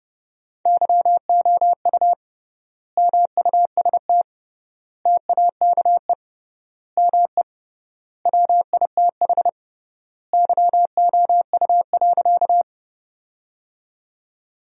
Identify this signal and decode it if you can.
Morse code, heard as a keyed tone, about 30 wpm, 700 Hz. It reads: YOU MUST TAKE ME WITH YOU.
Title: 3x Word-Spacing - Sentences from Top 200 Words - 30wpm